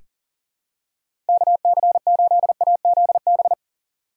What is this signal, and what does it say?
Morse code, heard as a keyed tone, about 40 wpm, 700 Hz. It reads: KC8AZB
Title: USA Call Signs - 6 Characters - 4x Character Spacing - 40wpm